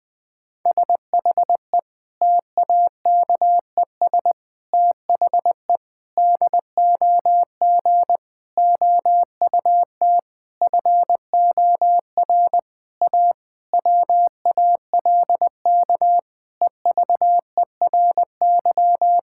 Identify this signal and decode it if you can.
Morse code, heard as a keyed tone, about 20 wpm, 700 Hz. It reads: SHE TAKES THE DOG OUT FOR A WALK EVERY
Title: Difficult Sentences with Hints - 20wpm